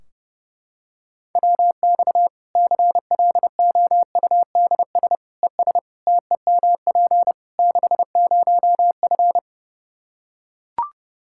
Morse code, heard as a keyed tone, about 32 wpm, 700 Hz. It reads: WX CLOUDS ES TEMP 60F
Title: Practice QSO - Exchange 2 - Info - 30wpm